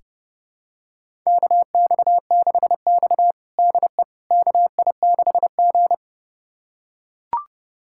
Morse code, heard as a keyed tone, about 30 wpm, 700 Hz. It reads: KX6X DE KI6G
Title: USA Callsign DE Callsign - 4 Characters - 30wpm